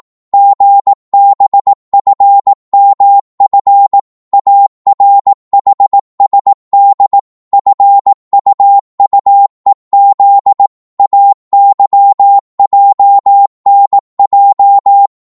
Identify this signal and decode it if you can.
Morse code, heard as a keyed tone, about 18 wpm, 800 Hz. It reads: GBFMF ARHSD FUUEZ AYJNJ
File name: Morse / ltr-18wpm